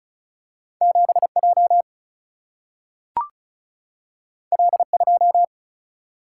Morse code, heard as a keyed tone, about 36 wpm, 700 Hz. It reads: 7J L2
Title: Single-Letter Single-Number 35wpm